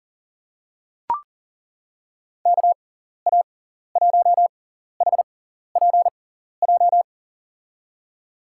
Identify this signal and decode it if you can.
Morse code, heard as a keyed tone, about 40 wpm, spaced as for 14 wpm, 700 Hz. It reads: KA1HPJ
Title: USA Call Signs - 6 Characters - 6x Character Spacing - 40wpm